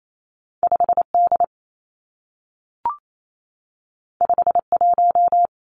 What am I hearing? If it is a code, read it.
Morse code, heard as a keyed tone, about 28 wpm, 700 Hz. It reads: 5 D 5 1
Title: Two Challenging Letter-Number Combinations 28wpm